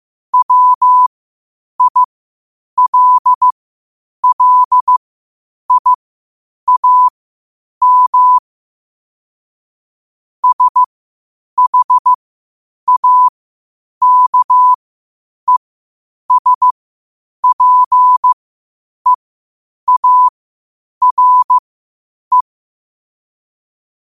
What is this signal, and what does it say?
Morse code, heard as a keyed tone, about 15 wpm, spaced as for 8 wpm, 1000 Hz. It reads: WILLIAM SHAKESPEARE
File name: qotdfiles / 8wpm